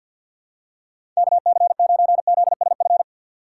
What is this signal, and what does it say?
Morse code, heard as a keyed tone, about 52 wpm, 700 Hz. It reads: KC9ZSF
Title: USA Call Signs - 3x Character Spacing - 50wpm